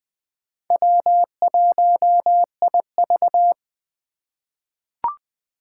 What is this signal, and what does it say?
Morse code, heard as a keyed tone, about 21 wpm, 700 Hz. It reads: W1IV
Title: Four Challenging Letter-Number Combinations 20wpm